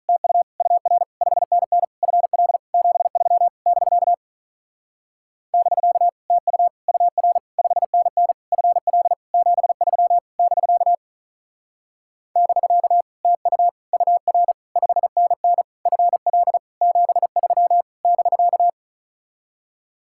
Morse code, heard as a keyed tone, about 45 wpm, 700 Hz. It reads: TU UR 5NN FL 73 <BK> <BK> TU UR 5NN FL 73 <BK> <BK> TU UR 5NN FL 73 <BK>